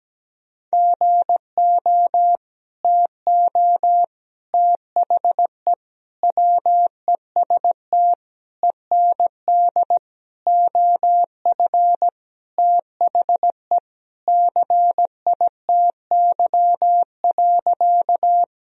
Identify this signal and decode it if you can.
Morse code, heard as a keyed tone, about 17 wpm, 700 Hz. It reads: GO TO THE WEST END OF THE CITY.
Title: Sentences from Top 1000 Words - Review Part 2 - 17wpm